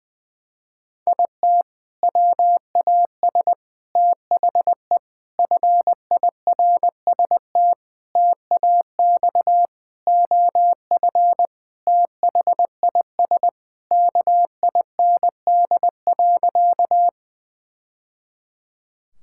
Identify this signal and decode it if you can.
Morse code, heard as a keyed tone, about 20 wpm, 700 Hz. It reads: IT WAS THE FIRST TAX OF THIS KIND.
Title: Sentences from Top 1000 Words - Review Part 5 - 20wpm